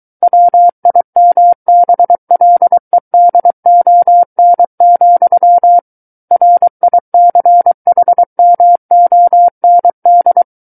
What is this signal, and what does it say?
Morse code, heard as a keyed tone, about 23 wpm, 700 Hz. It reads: WIMBLEDON, RICHMOND
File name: Book One / 14-Chapter 14 - 23 WPM